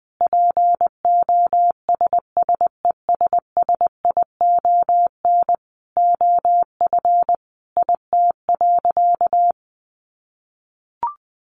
Morse code, heard as a keyed tone, about 20 wpm, 700 Hz. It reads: POSSESSION OF IT.
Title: Sentences from Top 1400 Words 20wpm